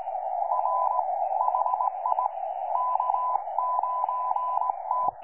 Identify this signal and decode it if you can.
Morse code, heard as a keyed tone, about 19 wpm, 1000 Hz, with radio noise. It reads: R HI K9T
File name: 12 39Z 18088.2kHz utah aa7fv k9tjl